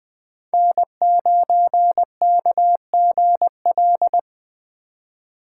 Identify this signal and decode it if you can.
Morse code, heard as a keyed tone, about 21 wpm, 700 Hz. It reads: N9KGL
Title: USA Call Signs - 2x Character Spacing - 20wpm